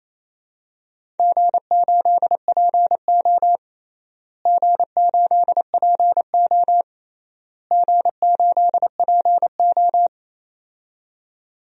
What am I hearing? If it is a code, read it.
Morse code, heard as a keyed tone, about 28 wpm, 700 Hz. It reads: G8PO G8PO G8PO
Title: DXCC-100 Call Signs - 4 Characters - Repeated 3x - 28wpm